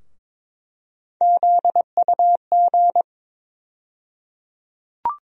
Morse code, heard as a keyed tone, about 22 wpm, 700 Hz. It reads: ZUG
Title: USA Call Sign Postfixes - 3 Letters 22wpm